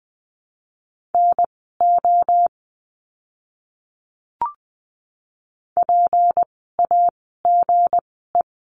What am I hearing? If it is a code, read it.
Morse code, heard as a keyed tone, about 20 wpm, spaced as for 15 wpm, 700 Hz. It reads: NO PAGE
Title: 160 Lesson 26f - Letter g - Words 20wpm